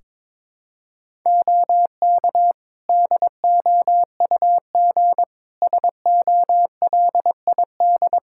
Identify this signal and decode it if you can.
Morse code, heard as a keyed tone, about 22 wpm, 700 Hz. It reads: OK DOUG SOLID
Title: K1USN SST QSO Practice 22wpm